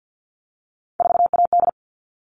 Morse code, heard as a keyed tone, about 50 wpm, 700 Hz. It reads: 4AD
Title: All Cut Numbers - 3-Digit 50wpm